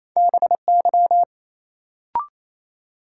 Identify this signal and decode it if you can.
Morse code, heard as a keyed tone, about 28 wpm, 700 Hz. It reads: BY
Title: Sentences from Top 100 Words - New Word Drill - 28wpm